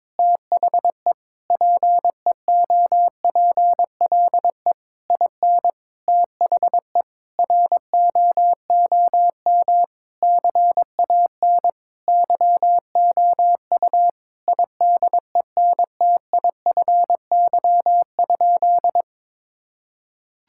Morse code, heard as a keyed tone, about 22 wpm, 700 Hz. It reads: THE PEOPLE IN THE ROOM CAN YOU IDENTIFY?
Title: Sentences from Top 1000 Words - Review Part 6 - 22wpm